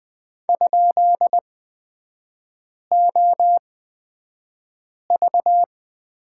Morse code, heard as a keyed tone, about 20 wpm, spaced as for 5 wpm, 700 Hz. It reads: ?OV
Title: 224 Lesson 36e - BT - Three Characters 20wpm